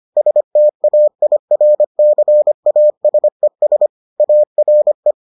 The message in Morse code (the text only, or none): STAIRCASES ARE